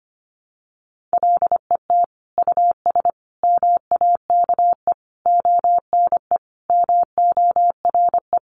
Morse code, heard as a keyed tone, about 25 wpm, 700 Hz. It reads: LET US MAKE ONE MORE
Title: Sentences from Top 700 Words 25wpm